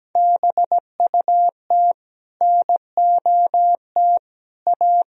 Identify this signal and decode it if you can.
Morse code, heard as a keyed tone, about 17 wpm, 700 Hz. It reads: BUT NOT A
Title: Sentences from Top 1000 Words - Review Part 4 - 17wpm